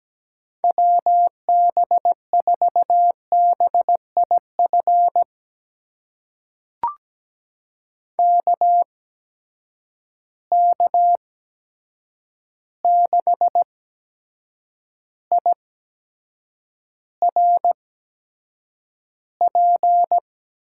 Morse code, heard as a keyed tone, about 17 wpm, spaced as for 7 wpm, 700 Hz. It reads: WB4BIF KK6IRP
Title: USA Call Signs - 6 Characters - 8x Character Spacing - 17wpm